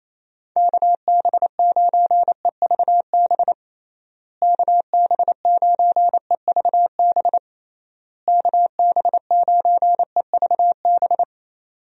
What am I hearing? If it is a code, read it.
Morse code, heard as a keyed tone, about 28 wpm, 700 Hz. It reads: KB9EVB KB9EVB KB9EVB
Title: USA Call Signs - 6 Characters - Repeated 3x - 28wpm